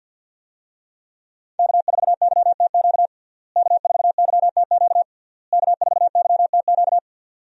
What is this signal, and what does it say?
Morse code, heard as a keyed tone, about 51 wpm, 700 Hz. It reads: K4YTQ K4YTQ K4YTQ
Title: USA Call Signs - 5 Characters - Repeated 3x - 50wpm